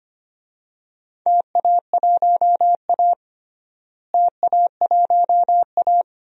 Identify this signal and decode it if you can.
Morse code, heard as a keyed tone, about 25 wpm, 700 Hz. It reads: TA1A TA1A
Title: DXCC-100 Call Signs - 4 Characters - Repeated 3x - 25wpm